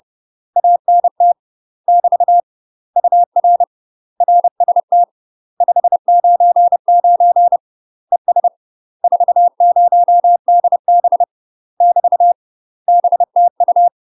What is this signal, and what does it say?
Morse code, heard as a keyed tone, about 30 wpm, 700 Hz. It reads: ANT <BT> UR RST 599 ES 40DB <BT> BTU
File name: UK MASTER QSOs TRAINING / 30WPM / UK MASTER QSOs 0000